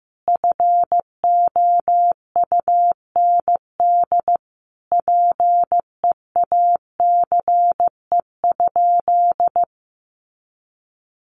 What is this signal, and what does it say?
Morse code, heard as a keyed tone, about 15 wpm, 700 Hz. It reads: FOUND PEACE?
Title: Sentences from Top 1500 Words 15wpm